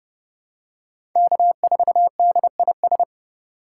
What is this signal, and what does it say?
Morse code, heard as a keyed tone, about 29 wpm, 700 Hz. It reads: K4DIS
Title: USA Call Signs - 5 Characters - 2x Character Spacing - 30wpm